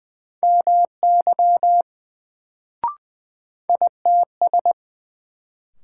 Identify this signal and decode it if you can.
Morse code, heard as a keyed tone, about 20 wpm, 700 Hz. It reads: MY ITS
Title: Sets of 2 Words - New Word Drill - 20wpm